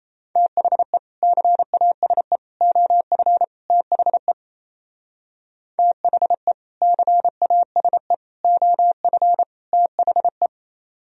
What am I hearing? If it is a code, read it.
Morse code, heard as a keyed tone, about 34 wpm, 700 Hz. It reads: THE CASE OF THE THE CASE OF THE
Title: Speed-Racing - Sets of 4 Words 28wpm